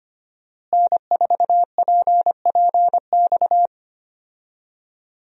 Morse code, heard as a keyed tone, about 25 wpm, 690 Hz. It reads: N4PPX
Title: USA Call Signs - 5 Characters - Repeated 2x - 25wpm